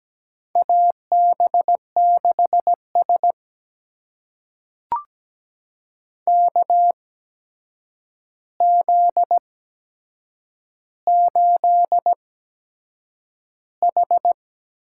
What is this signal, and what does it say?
Morse code, heard as a keyed tone, about 17 wpm, spaced as for 7 wpm, 700 Hz. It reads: AB6S KZ8H
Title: USA Call Signs - 4 Characters - 8x Character Spacing - 17wpm